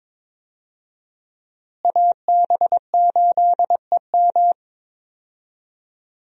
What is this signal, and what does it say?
Morse code, heard as a keyed tone, about 22 wpm, 700 Hz. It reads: AB8EM
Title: USA Call Signs - 5 Characters - Repeated 2x - 22wpm